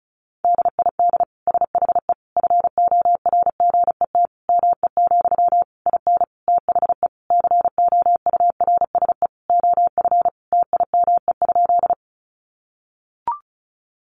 Morse code, heard as a keyed tone, about 34 wpm, 700 Hz. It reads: DID SHE FORGET ME, IN THE COURSE OF TIME?
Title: Sentences from Top 1200 Words 35wpm